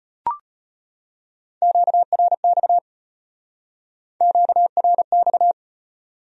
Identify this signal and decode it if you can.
Morse code, heard as a keyed tone, about 35 wpm, 700 Hz. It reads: QRX QRX